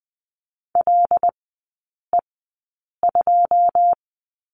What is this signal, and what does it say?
Morse code, heard as a keyed tone, about 20 wpm, spaced as for 8 wpm, 700 Hz. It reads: LE2
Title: 093 Lesson 16e - Letter c - Three Characters 20wpm